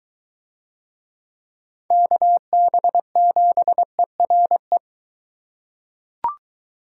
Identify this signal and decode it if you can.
Morse code, heard as a keyed tone, about 22 wpm, 700 Hz. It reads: KB7ERE